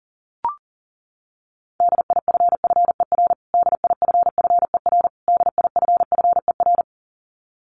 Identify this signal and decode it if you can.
Morse code, heard as a keyed tone, about 40 wpm, 700 Hz. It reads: DIFFER DIFFER DIFFER
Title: Top 200 Words Repeated 40wpm